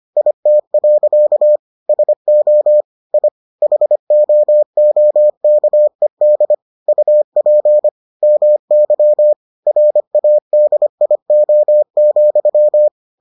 Morse code, heard as a keyed tone, about 25 wpm, 600 Hz. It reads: IT. SO I HOOKED UP MY RADIO,